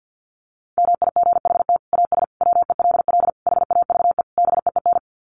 Morse code, heard as a keyed tone, about 50 wpm, 700 Hz. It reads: MIGHT AS WELL HAVE BEEN
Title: Sets of 5 Words Encore2 50wpm